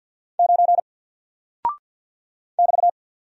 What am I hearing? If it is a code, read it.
Morse code, heard as a keyed tone, about 50 wpm, 700 Hz. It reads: 9 <BT>
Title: Single Character (Letter:Number:Punctuation:Prosign) 50wpm